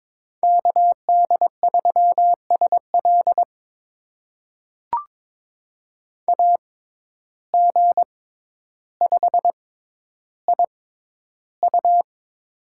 Morse code, heard as a keyed tone, about 22 wpm, spaced as for 11 wpm, 700 Hz. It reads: KD3SL AG5IU
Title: USA Call Signs - 5 Characters - 6x Character Spacing - 22wpm